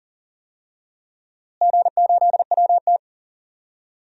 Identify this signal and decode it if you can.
Morse code, heard as a keyed tone, about 40 wpm, 700 Hz. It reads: G8WT
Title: DXCC-100 Call Signs - 4 Characters - Repeated 2x - 40wpm